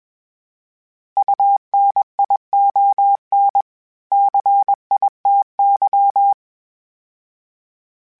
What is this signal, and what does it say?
Morse code, heard as a keyed tone, about 21 wpm, 800 Hz. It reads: UNION CITY
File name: fromSMs / FromSMs mp3s / Cities@20